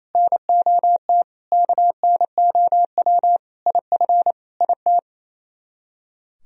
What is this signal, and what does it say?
Morse code, heard as a keyed tone, about 34 wpm, 700 Hz. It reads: NOT KNOW IF IT
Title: Speed-Racing - Sets of 5 Words 28wpm